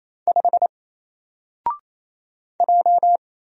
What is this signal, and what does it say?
Morse code, heard as a keyed tone, about 28 wpm, 700 Hz. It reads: 5 J
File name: Single-Letter-Number-28wpm